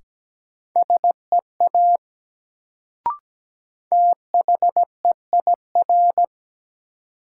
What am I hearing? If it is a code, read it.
Morse code, heard as a keyed tone, about 17 wpm, 700 Hz. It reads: SEA THEIR
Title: Sentences from Top 300 Words - New Word Drill - 17wpm